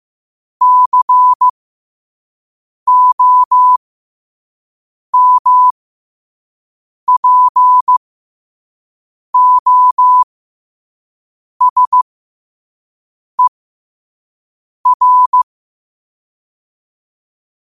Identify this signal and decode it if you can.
Morse code, heard as a keyed tone, about 15 wpm, spaced as for 5 wpm, 1000 Hz. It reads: COMPOSER
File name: qotdfiles / 5wpm